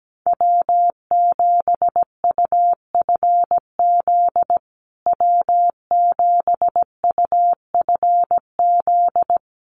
Morse code, heard as a keyed tone, about 17 wpm, 700 Hz. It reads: W7UFZ W7UFZ
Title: USA Call Signs Repeated 3x 17wpm